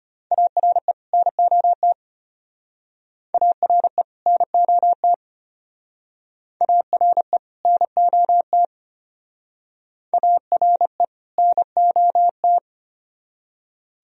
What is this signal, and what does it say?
Morse code, heard as a keyed tone, about 33 wpm, 700 Hz. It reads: ARE NOT ARE NOT ARE NOT ARE NOT